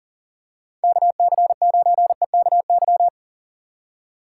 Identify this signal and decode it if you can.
Morse code, heard as a keyed tone, about 40 wpm, 700 Hz. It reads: KC9EKY
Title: USA Call Signs - 6 Characters - 2x Character Spacing - 40wpm